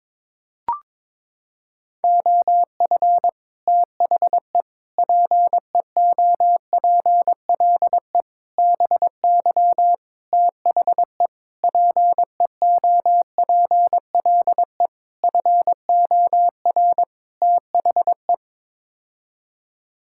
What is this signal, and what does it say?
Morse code, heard as a keyed tone, about 22 wpm, 700 Hz. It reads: OF THE PEOPLE BY THE PEOPLE FOR THE
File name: practice / 22wpm / Sets of 8 Words - 22wpm